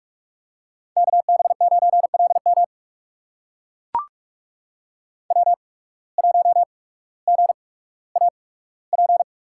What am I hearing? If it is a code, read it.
Morse code, heard as a keyed tone, about 43 wpm, spaced as for 19 wpm, 700 Hz. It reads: KB9LM W1GAP
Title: USA Call Signs - 5 Characters - 8x Character Spacing - 45wpm